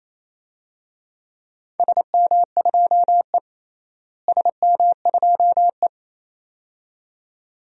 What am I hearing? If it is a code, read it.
Morse code, heard as a keyed tone, about 28 wpm, 700 Hz. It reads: SM2E SM2E
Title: DXCC-100 Call Signs - 4 Characters - Repeated 2x - 28wpm